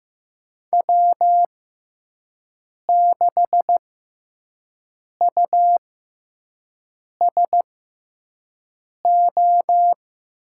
Morse code, heard as a keyed tone, about 15 wpm, spaced as for 5 wpm, 700 Hz. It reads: W6USO
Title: USA Call Signs - 6x Character Spacing - 15wpm